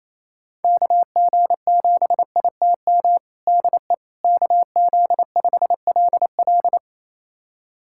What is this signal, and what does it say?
Morse code, heard as a keyed tone, about 28 wpm, 700 Hz. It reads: KG7ITM DE KZ5LL
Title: USA Callsign DE Callsign 28wpm